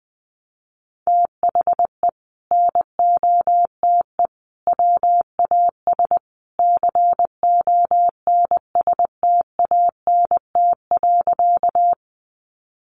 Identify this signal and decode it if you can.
Morse code, heard as a keyed tone, about 20 wpm, 700 Hz. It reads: THE NOTE WAS CONSTANT.
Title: Sentences from Top 1500 Words 20wpm